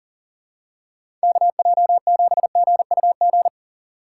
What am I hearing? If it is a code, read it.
Morse code, heard as a keyed tone, about 38 wpm, 700 Hz. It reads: KJ7GUG
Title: USA Call Signs - 6 Characters - 3x Character Spacing - 40wpm